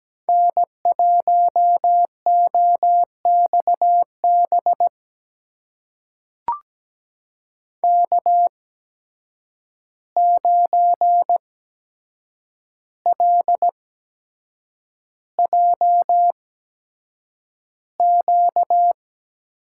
Morse code, heard as a keyed tone, about 17 wpm, spaced as for 7 wpm, 700 Hz. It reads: N1OXB K9LJQ
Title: USA Call Signs - 5 Characters - 8x Character Spacing - 17wpm